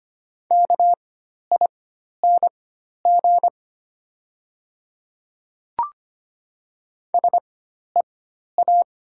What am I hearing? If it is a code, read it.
Morse code, heard as a keyed tone, about 25 wpm, spaced as for 12 wpm, 700 Hz. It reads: KING SEA
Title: 4x Character Spacing - Top 500 Words - 25wpm